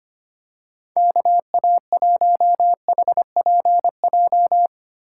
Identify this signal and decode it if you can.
Morse code, heard as a keyed tone, about 25 wpm, 700 Hz. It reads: KA1HPJ
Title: USA Call Signs - 6 Characters - 4x Character Spacing - 25wpm